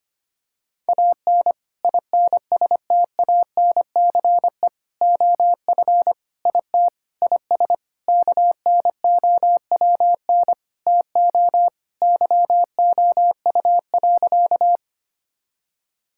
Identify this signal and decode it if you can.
Morse code, heard as a keyed tone, about 25 wpm, 700 Hz. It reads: AN INSTANCE OF IT IS KNOWN TO YOU.
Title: Sentences from Top 1000 Words - Review Part 5 - 25wpm